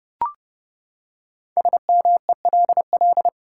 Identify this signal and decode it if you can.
Morse code, heard as a keyed tone, about 32 wpm, 700 Hz. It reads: SMELL
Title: Five Letter Words - 30wpm